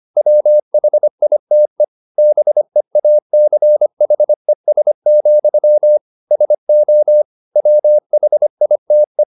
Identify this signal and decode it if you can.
Morse code, heard as a keyed tone, about 25 wpm, 600 Hz. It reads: WHITE BEACHES, SO WHITE
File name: the old man and the sea part 1 25wpm